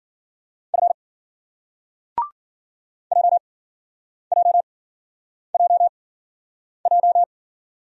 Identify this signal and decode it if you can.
Morse code, heard as a keyed tone, about 54 wpm, 700 Hz. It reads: F J J J J